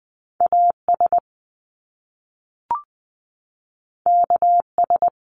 Morse code, heard as a keyed tone, about 20 wpm, 700 Hz. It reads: AS KS
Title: Top Two Letter Combinations in Words 20wpm